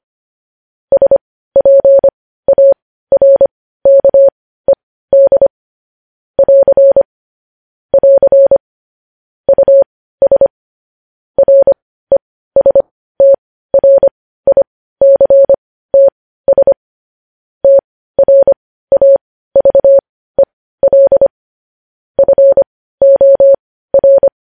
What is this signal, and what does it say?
Morse code, heard as a keyed tone, about 25 wpm, spaced as for 15 wpm, 550 Hz. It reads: SPARKED <AR> <AR> US RESTRICTS TRAVEL FOR
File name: podcast / News headlines-1x-25x15